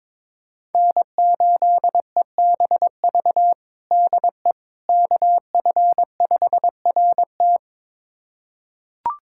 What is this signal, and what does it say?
Morse code, heard as a keyed tone, about 21 wpm, 700 Hz. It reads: N8EBV DE KF5RT
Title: USA Callsign DE Callsign - 5 Characters - 22wpm